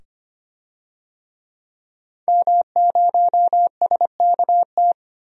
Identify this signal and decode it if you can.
Morse code, heard as a keyed tone, about 25 wpm, 700 Hz. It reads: M0SKT
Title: DXCC-100 Call Signs - 5 Characters - 25wpm